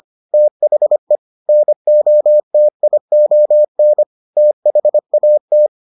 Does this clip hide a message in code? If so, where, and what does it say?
Yes: Morse code from 0.3 to 5.7 s, heard as a keyed tone, about 25 wpm, 600 Hz. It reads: THE NOTION THAT